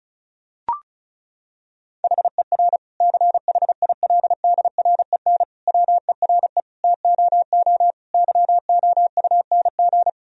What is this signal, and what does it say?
Morse code, heard as a keyed tone, about 36 wpm, 700 Hz. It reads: HER CHILDREN WERE TOO YOUNG
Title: Sentences from Top 1000 Words - Review Part 4 - 35wpm